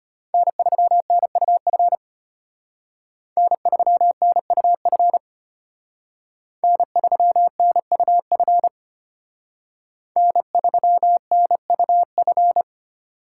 Morse code, heard as a keyed tone, about 33 wpm, 700 Hz. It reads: N3NUF N3NUF N3NUF N3NUF